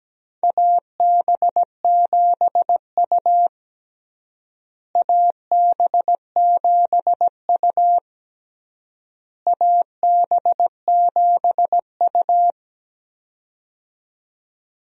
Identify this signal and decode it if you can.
Morse code, heard as a keyed tone, about 17 wpm, 700 Hz. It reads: AB7U AB7U AB7U
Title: USA Call Signs - 4 Characters - Repeated 3x - 17wpm